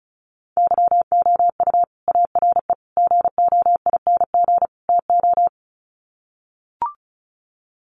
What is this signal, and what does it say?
Morse code, heard as a keyed tone, about 34 wpm, 700 Hz. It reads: YOU ARE GOING TO
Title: Sets of 4 Words 35wpm